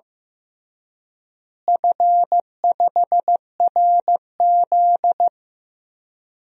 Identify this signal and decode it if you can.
Morse code, heard as a keyed tone, about 15 wpm, 700 Hz. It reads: F5RZ
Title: DXCC-100 Call Signs - 4 Characters - Repeated 3x - 15wpm